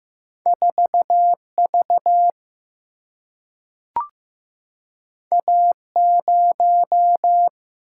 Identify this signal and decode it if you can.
Morse code, heard as a keyed tone, about 15 wpm, 700 Hz. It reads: 4V A0
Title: Single-Letter Single-Number 15wpm